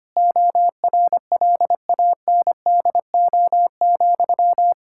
Morse code, heard as a keyed tone, about 25 wpm, 700 Hz. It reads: ORLANDO,
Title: Practice QSO - Exchange 1 - Beginning - 25wpm